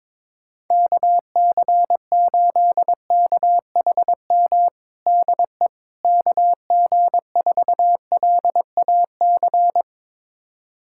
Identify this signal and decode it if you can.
Morse code, heard as a keyed tone, about 22 wpm, 700 Hz. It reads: KC8KHM DE KG4LAC
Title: USA Callsign DE Callsign - 6 Characters - 22wpm